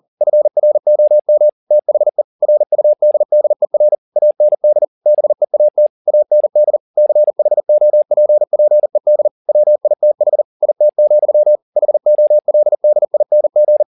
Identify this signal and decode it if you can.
Morse code, heard as a keyed tone, about 40 wpm, 600 Hz. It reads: FROM THE RUDDER AND BEAT AND CHOPPED WITH IT, HOLDING
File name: the old man and the sea 2 40wpm